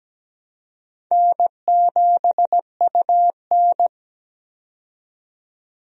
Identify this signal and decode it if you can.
Morse code, heard as a keyed tone, about 17 wpm, 700 Hz. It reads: N7UN
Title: USA Call Signs - 4 Characters - 17wpm